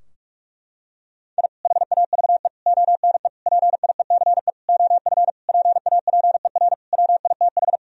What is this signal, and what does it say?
Morse code, heard as a keyed tone, about 46 wpm, 700 Hz. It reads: I HAVE ONE PIECE OF PAPER WITH
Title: Sentences from Top 1000 Words - Review Part 2 - 45wpm